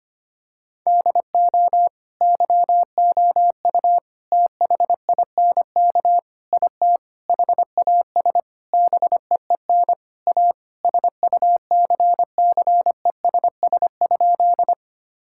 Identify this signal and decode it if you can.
Morse code, heard as a keyed tone, about 25 wpm, 700 Hz. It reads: DO YOU THINK IT HAS BEEN A SUCCESS?
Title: Sentences from Top 1000 Words - Review Part 6 - 25wpm